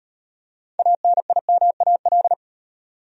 Morse code, heard as a keyed tone, about 38 wpm, 700 Hz. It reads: ANIMAL